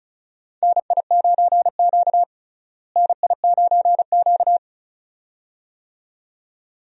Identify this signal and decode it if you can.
Morse code, heard as a keyed tone, about 36 wpm, 700 Hz. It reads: NI9Q NI9Q
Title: USA Call Signs - 4 Characters - Repeated 2x - 35wpm